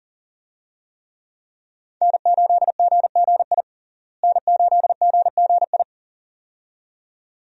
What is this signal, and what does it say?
Morse code, heard as a keyed tone, about 40 wpm, 700 Hz. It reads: N8GGI N8GGI
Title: USA Call Signs - 5 Characters - Repeated 2x - 40wpm